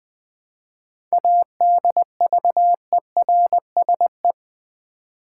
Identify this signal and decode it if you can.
Morse code, heard as a keyed tone, about 20 wpm, 700 Hz. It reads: ADVERSE
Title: 7 Letter Words - First 3 Letters Given - 20wpm